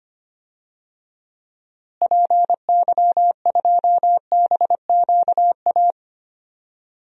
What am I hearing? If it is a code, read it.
Morse code, heard as a keyed tone, about 25 wpm, 700 Hz. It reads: PY2BQA
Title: DXCC-100 Call Signs - 6 Characters - Repeated 3x - 25wpm